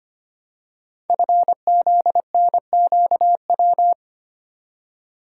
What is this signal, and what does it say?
Morse code, heard as a keyed tone, about 25 wpm, 700 Hz. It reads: FZNQW
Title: Code Groups - 5 Letters - 25wpm